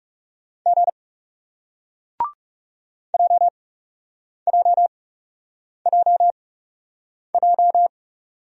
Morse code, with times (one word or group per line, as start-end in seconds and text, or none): 0.7-0.9 G
3.1-7.9 JJJJ